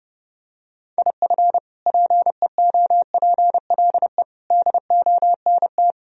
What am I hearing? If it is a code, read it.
Morse code, heard as a keyed tone, about 32 wpm, 700 Hz. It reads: IF PEOPLE DONT
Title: Sentences from Top 1000 Words - Review Part 3 - 30wpm